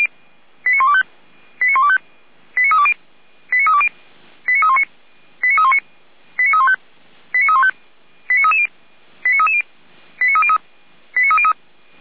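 Five-tone selective call (ZVEI), heard as quick runs of five tones, tones of about 70 ms. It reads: 89125 89125 89320 89320 89319 89319 89315 89315 89330 89330 89303 89303
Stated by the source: ZVEI five-tone selective call received on SDR.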